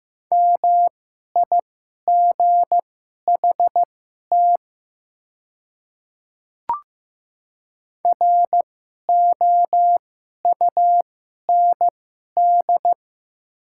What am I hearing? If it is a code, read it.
Morse code, heard as a keyed tone, about 15 wpm, spaced as for 11 wpm, 700 Hz. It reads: MIGHT ROUND
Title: 2x Character Spacing - Top 300 Words - 15wpm